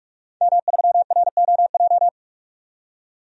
Morse code, heard as a keyed tone, about 46 wpm, 700 Hz. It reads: M3ROJ
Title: DXCC-100 Call Signs - 5 Characters - 45wpm